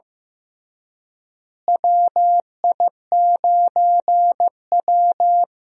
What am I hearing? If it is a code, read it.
Morse code, heard as a keyed tone, about 15 wpm, 700 Hz. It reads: WI9W
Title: USA Call Signs - 4 Characters - Repeated 3x - 15wpm